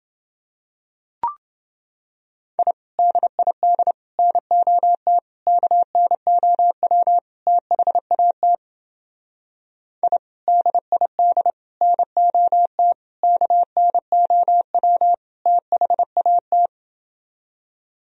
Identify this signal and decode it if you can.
Morse code, heard as a keyed tone, about 29 wpm, 700 Hz. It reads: I DID NOT KNOW THAT I DID NOT KNOW THAT